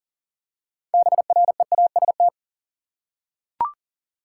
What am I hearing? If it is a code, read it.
Morse code, heard as a keyed tone, about 38 wpm, 700 Hz. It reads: BREAST
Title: Six Letter Words - 40wpm